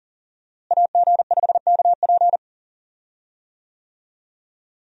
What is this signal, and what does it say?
Morse code, heard as a keyed tone, about 40 wpm, 700 Hz. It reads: AG5KP
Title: USA Call Signs - 5 Characters - Repeated 2x - 40wpm